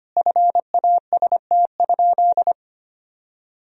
Morse code, heard as a keyed tone, about 25 wpm, 700 Hz. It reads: FAST?
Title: Sentences from Top 1000 Words - Review Part 2 - 25wpm